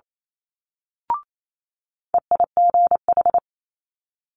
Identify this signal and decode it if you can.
Morse code, heard as a keyed tone, about 28 wpm, 700 Hz. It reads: EIGH
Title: Multi-letter Phonemes Quick Fire 28wpm